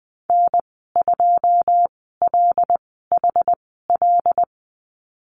Morse code, heard as a keyed tone, about 20 wpm, spaced as for 15 wpm, 700 Hz. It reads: N2LHL
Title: 122 Lesson 20g - Number 3 - Call Signs 20wpm